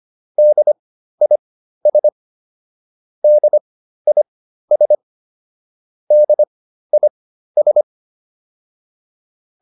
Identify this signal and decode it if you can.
Morse code, heard as a keyed tone, about 25 wpm, spaced as for 13 wpm, 600 Hz. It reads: DIS DIS DIS